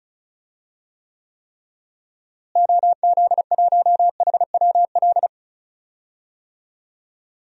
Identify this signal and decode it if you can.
Morse code, heard as a keyed tone, about 36 wpm, 700 Hz. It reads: OZ1HWL
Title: DXCC-100 Call Signs - 6 Characters - Repeated 3x - 35wpm